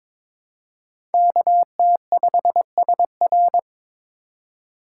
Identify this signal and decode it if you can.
Morse code, heard as a keyed tone, about 22 wpm, 700 Hz. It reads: KT5SR
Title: USA Call Signs - 5 Characters - 2x Character Spacing - 22wpm